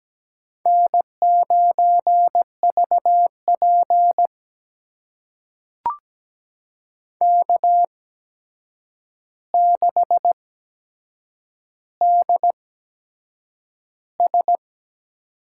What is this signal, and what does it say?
Morse code, heard as a keyed tone, about 17 wpm, spaced as for 7 wpm, 700 Hz. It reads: N9VP K6DS
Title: USA Call Signs - 4 Characters - 8x Character Spacing - 17wpm